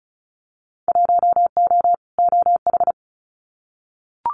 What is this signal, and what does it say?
Morse code, heard as a keyed tone, about 34 wpm, 700 Hz. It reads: 1O OH
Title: Winter Field Day Exchanges 35wpm